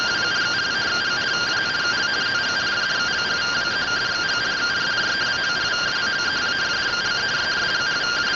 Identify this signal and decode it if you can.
Two-tone teleprinter signal, about 170 Hz shift.
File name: findings / AXIAL fr45 3mm.gen-26 rtty rx